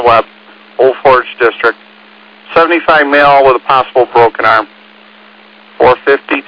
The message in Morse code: none

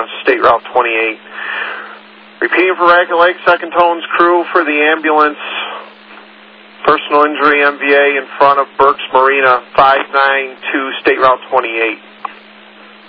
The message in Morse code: none

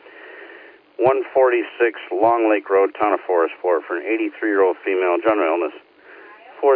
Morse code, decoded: none

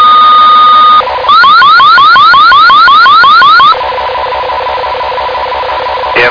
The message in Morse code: none

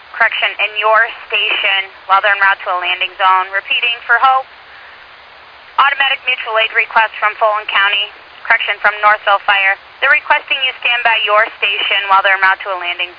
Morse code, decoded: none

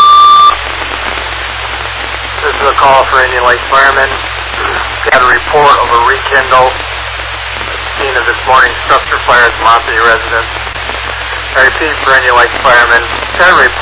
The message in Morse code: none

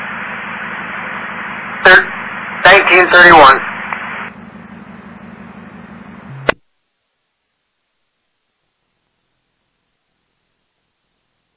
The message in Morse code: none

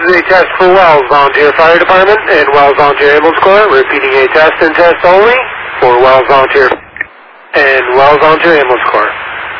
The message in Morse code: none

